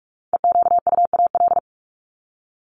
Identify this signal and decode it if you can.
Morse code, heard as a keyed tone, about 44 wpm, 700 Hz. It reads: EQUAL
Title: LY Suffix - Top 100 Words with Base Words 45wpm